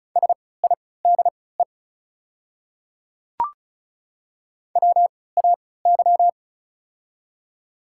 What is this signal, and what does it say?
Morse code, heard as a keyed tone, about 36 wpm, spaced as for 20 wpm, 700 Hz. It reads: SIDE WAY
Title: 3x Character Spacing - Top 200 Words - 35wpm